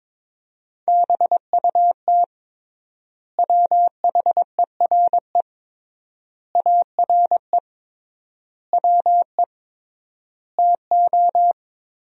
Morse code, heard as a keyed tone, about 22 wpm, 700 Hz. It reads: BUT WHERE ARE WE TO
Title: 3x Word-Spacing - Sentences from Top 200 Words - 22wpm